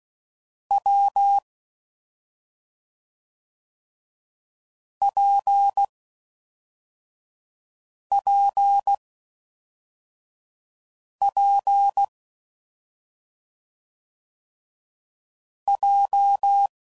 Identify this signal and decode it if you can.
Morse code, heard as a keyed tone, about 16 wpm, 800 Hz. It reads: W P P P J
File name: Combinations / Difficult Combos / WPJ WPJ1